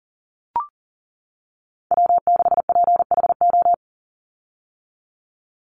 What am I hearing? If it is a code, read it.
Morse code, heard as a keyed tone, about 40 wpm, 700 Hz. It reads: W6PHO
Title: Call Signs - 40wpm